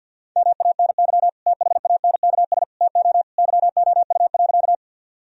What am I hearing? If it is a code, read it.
Morse code, heard as a keyed tone, about 49 wpm, 700 Hz. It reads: MANY THANKS TO YOU.
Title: Sentences from Top 1000 Words - Review Part 3 - 50wpm